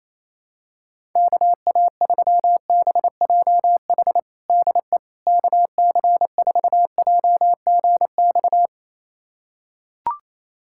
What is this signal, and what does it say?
Morse code, heard as a keyed tone, about 28 wpm, 700 Hz. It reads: KA3BJH DE KC4JGX
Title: USA Callsign DE Callsign - 6 Characters - 28wpm